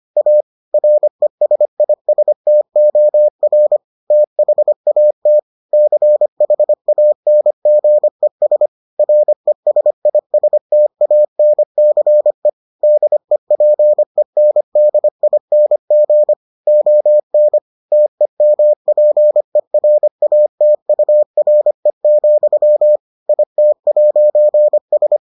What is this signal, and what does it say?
Morse code, heard as a keyed tone, about 25 wpm, 600 Hz. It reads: A RESISTOR THAT CHANGES RESISTANCE DEPENDING ON TEMPERATURE, IT'S